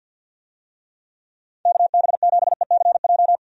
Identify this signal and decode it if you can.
Morse code, heard as a keyed tone, about 53 wpm, 700 Hz. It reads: KB7ECJ
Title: USA Call Signs - 6 Characters - 50wpm